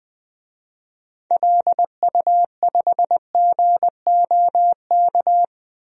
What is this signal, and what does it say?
Morse code, heard as a keyed tone, about 21 wpm, 700 Hz. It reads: LU5GOK
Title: DXCC-100 Call Signs - 6 Characters - Repeated 3x - 20wpm